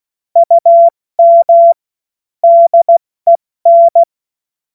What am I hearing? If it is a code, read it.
Morse code, heard as a keyed tone, about 16 wpm, 680 Hz. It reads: UM DEN
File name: Morsekurs/Witze